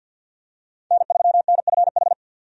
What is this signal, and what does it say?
Morse code, heard as a keyed tone, about 50 wpm, 700 Hz. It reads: N3NFH
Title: USA Call Signs - 5 Characters - 50wpm